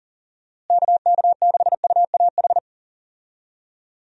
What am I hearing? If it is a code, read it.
Morse code, heard as a keyed tone, about 38 wpm, 700 Hz. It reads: KK6UAH
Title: USA Call Signs - 6 Characters - Repeated 3x - 40wpm